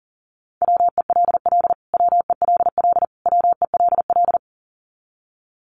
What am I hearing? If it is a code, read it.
Morse code, heard as a keyed tone, about 40 wpm, 700 Hz. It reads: WELL WELL WELL
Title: Top 200 Words Repeated 40wpm